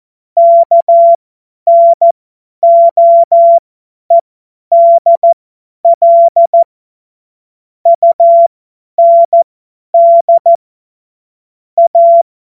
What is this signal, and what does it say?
Morse code, heard as a keyed tone, about 14 wpm, spaced as for 10 wpm, 680 Hz. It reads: KNOEDL UND A
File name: Witz-19-August-2025-50-F70